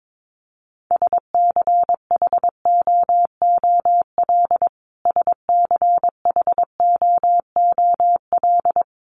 Morse code, heard as a keyed tone, about 22 wpm, 700 Hz. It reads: SCHOOL SCHOOL
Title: Top 500 Words Repeated 22wpm